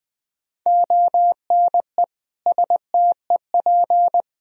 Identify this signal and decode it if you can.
Morse code, heard as a keyed tone, about 20 wpm, 700 Hz. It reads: ONE STEP
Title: Sentences from Top 1000 Words - Review Part 3 - 20wpm